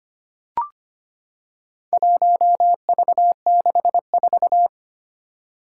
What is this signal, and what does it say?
Morse code, heard as a keyed tone, about 25 wpm, 700 Hz. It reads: 1V64
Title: Four Challenging Letter-Number Combinations 25wpm